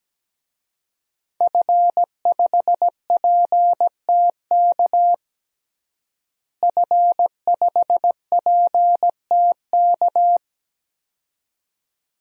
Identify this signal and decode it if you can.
Morse code, heard as a keyed tone, about 17 wpm, 700 Hz. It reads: F5PTK F5PTK
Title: DXCC-100 Call Signs - 5 Characters - Repeated 2x - 17wpm